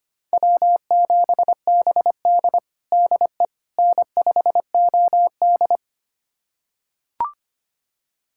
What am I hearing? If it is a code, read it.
Morse code, heard as a keyed tone, about 25 wpm, 700 Hz. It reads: W7BD DE N5OD
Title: USA Callsign DE Callsign - 4 Characters - 25wpm